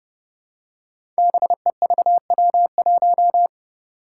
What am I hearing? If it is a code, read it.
Morse code, heard as a keyed tone, about 32 wpm, 700 Hz. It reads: BEVW1
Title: Five Challenging Letter-Number Combinations 30wpm